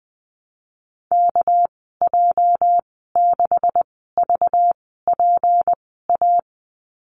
Morse code, heard as a keyed tone, about 19 wpm, spaced as for 14 wpm, 700 Hz. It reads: KJ6VPA
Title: 238 Lesson 38g - Letter q - Call Signs 20wpm